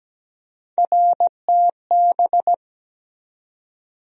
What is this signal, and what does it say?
Morse code, heard as a keyed tone, about 17 wpm, 700 Hz. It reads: RTB
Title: USA Call Sign Postfixes - 3 Letters 17wpm